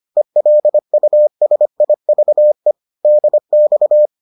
Morse code, heard as a keyed tone, about 25 wpm, 600 Hz. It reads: ELUSIVE DX